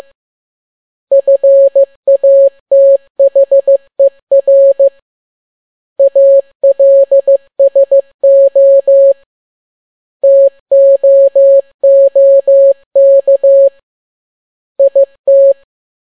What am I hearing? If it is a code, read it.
Morse code, heard as a keyed tone, about 15 wpm, 550 Hz. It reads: FATHER ALSO TOOK IT
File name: n6tt p1 15C13